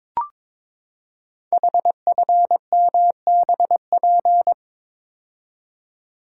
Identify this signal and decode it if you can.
Morse code, heard as a keyed tone, about 22 wpm, 700 Hz. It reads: HFMBP
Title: Code Groups - 5 Letter:Numbers - 22wpm